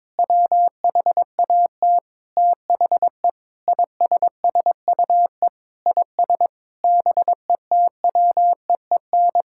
Morse code, heard as a keyed tone, about 22 wpm, 700 Hz. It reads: WHAT THE ISSUE IS BETWEEN
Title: Sentences from Top 1000 Words - Review Part 6 - 22wpm